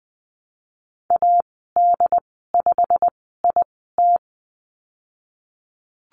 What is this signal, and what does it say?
Morse code, heard as a keyed tone, about 20 wpm, spaced as for 15 wpm, 700 Hz. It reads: AD5IT
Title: 147 Lesson 24g - Letter y - Call Signs 20wpm